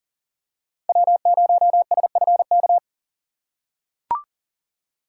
Morse code, heard as a keyed tone, about 40 wpm, 700 Hz. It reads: W0SFK
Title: USA Call Signs - 5 Characters - 2x Character Spacing - 40wpm